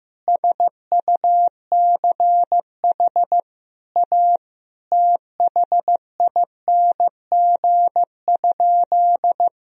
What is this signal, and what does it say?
Morse code, heard as a keyed tone, about 15 wpm, 700 Hz. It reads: SUCH A THING?
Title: Sentences from Top 200 Words and Abbreviations 15wpm